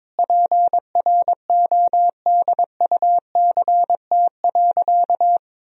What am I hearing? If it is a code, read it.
Morse code, heard as a keyed tone, about 22 wpm, 700 Hz. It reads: PRODUCT.
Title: Sentences from Top 1000 Words - Review Part 2 - 22wpm